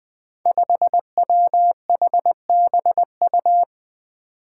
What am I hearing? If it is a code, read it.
Morse code, heard as a keyed tone, about 20 wpm, 700 Hz. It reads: 5WHBU
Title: Five Challenging Letter-Number Combinations 20wpm